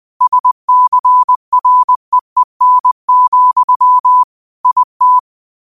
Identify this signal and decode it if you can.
Morse code, heard as a keyed tone, about 20 wpm, 1000 Hz. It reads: SCREEN, IT